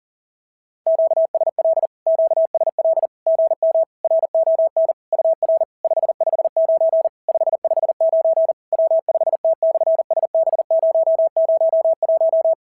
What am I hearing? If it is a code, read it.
Morse code, heard as a keyed tone, about 40 wpm, 650 Hz. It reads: QSL QSL GM RON UR 559 559 W5T/SB001
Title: SOTA QSO Practice 40wpm